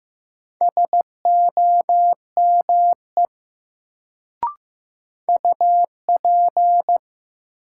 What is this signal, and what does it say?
Morse code, heard as a keyed tone, about 15 wpm, 700 Hz. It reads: SOME UP
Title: Sets of 2 Words - New Word Drill - 15wpm